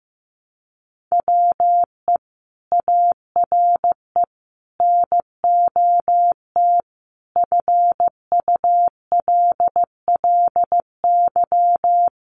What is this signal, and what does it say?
Morse code, heard as a keyed tone, about 15 wpm, 700 Hz. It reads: WE ARE NOT FULLY
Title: Sentences from Top 1300 Words 15wpm